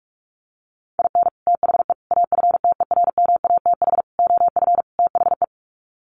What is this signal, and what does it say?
Morse code, heard as a keyed tone, about 44 wpm, 700 Hz. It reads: IN THE AFTERMATH OF THE
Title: Sets of 5 Words 45wpm